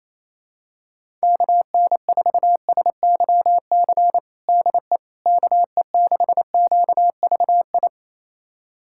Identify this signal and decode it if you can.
Morse code, heard as a keyed tone, about 28 wpm, 700 Hz. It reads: KN4SYC DE KE6QVI
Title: USA Callsign DE Callsign - 6 Characters - 28wpm